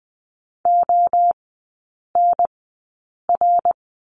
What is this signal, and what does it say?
Morse code, heard as a keyed tone, about 20 wpm, spaced as for 8 wpm, 700 Hz. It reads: ONR
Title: 074 Lesson 13e - Letter L - Three Characters 20wpm